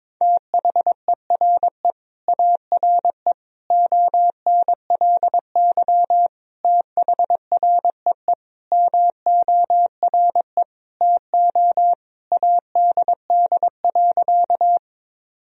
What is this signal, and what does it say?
Morse code, heard as a keyed tone, about 22 wpm, 700 Hz. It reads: THERE ARE ONLY THREE MORE TO ADD.
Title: Sentences from Top 1000 Words - Review Part 1 - 22wpm